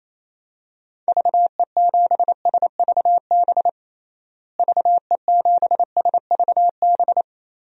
Morse code, heard as a keyed tone, about 28 wpm, 700 Hz. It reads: VE7SVB VE7SVB
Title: DXCC-100 Call Signs - 6 Characters - Repeated 2x - 28wpm